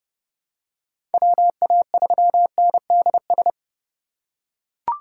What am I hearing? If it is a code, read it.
Morse code, heard as a keyed tone, about 30 wpm, 700 Hz. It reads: WA3NDS
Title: USA Call Signs - 6 Characters - 3x Character Spacing - 30wpm